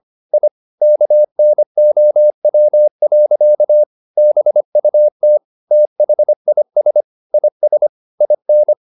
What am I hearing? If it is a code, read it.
Morse code, heard as a keyed tone, about 25 wpm, 600 Hz. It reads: I KNOW. BUT THIS IS IN
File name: the old man and the sea part 1 25wpm